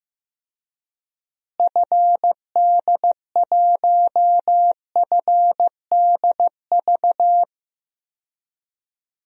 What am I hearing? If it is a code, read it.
Morse code, heard as a keyed tone, about 16 wpm, 700 Hz. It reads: FD1FDV
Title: DXCC-100 Call Signs - 6 Characters - Repeated 3x - 15wpm